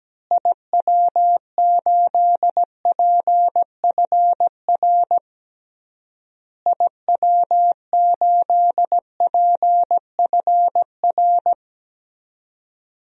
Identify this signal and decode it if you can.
Morse code, heard as a keyed tone, about 17 wpm, 700 Hz. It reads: IW8PFR IW8PFR
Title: DXCC-100 Call Signs - 6 Characters - Repeated 3x - 17wpm